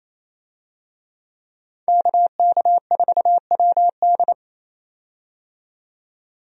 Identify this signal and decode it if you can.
Morse code, heard as a keyed tone, about 27 wpm, 700 Hz. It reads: KK4WD
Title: USA Call Signs - 5 Characters - 28wpm